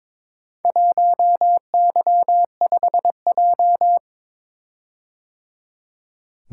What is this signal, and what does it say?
Morse code, heard as a keyed tone, about 22 wpm, 690 Hz. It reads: 1Y5J
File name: practice / 22wpm / Four Letter-Number Combinations - 22wpm